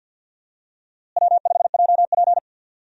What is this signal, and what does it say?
Morse code, heard as a keyed tone, about 48 wpm, 700 Hz. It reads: W5JP
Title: USA Call Signs - 4 Characters - Repeated 3x - 50wpm